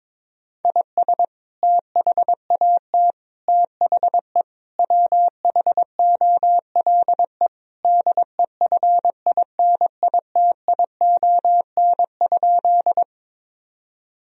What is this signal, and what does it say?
Morse code, heard as a keyed tone, about 22 wpm, 700 Hz. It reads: IS THAT THE WHOLE DEFINITION?
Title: Sentences from Top 1000 Words - Review Part 4 - 22wpm